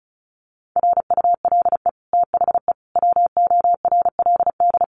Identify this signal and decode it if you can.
Morse code, heard as a keyed tone, about 34 wpm, 700 Hz. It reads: RULE THE WORLD
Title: Sentences from Top 600 Words 35wpm